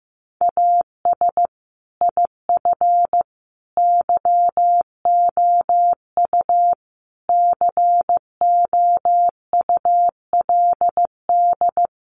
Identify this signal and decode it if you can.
Morse code, heard as a keyed tone, about 15 wpm, 700 Hz. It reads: AS IF YOU COULD
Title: Sentences from Top 900 Words 15wpm